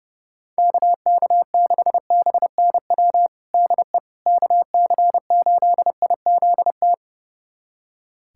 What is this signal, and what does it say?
Morse code, heard as a keyed tone, about 32 wpm, 700 Hz. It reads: KK6BNW DE KC8IZT
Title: USA Callsign DE Callsign 30wpm